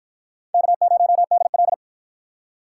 Morse code, heard as a keyed tone, about 52 wpm, 700 Hz. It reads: K0DL